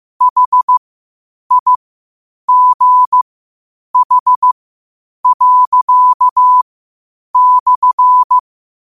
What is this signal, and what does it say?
Morse code, heard as a keyed tone, about 15 wpm, spaced as for 8 wpm, 1000 Hz. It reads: HIGH./
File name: qotdfiles / 8wpm